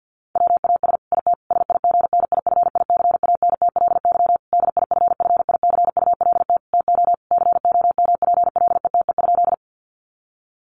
Morse code, heard as a keyed tone, about 50 wpm, 700 Hz. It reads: WAS IT SIGNIFICANTLY DIFFICULT TO COMPLETE?
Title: Sentences from Top 1500 Words 50wpm